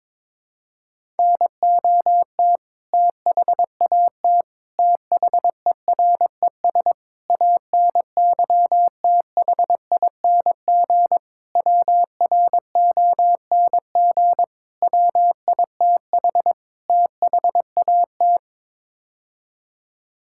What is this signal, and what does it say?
Morse code, heard as a keyed tone, about 22 wpm, 700 Hz. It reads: NOT THAT THERES ANYTHING WRONG WITH THAT
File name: practice / 22wpm / Sets of 7 Words - 22wpm